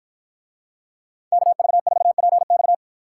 Morse code, heard as a keyed tone, about 49 wpm, 700 Hz. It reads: KV4PX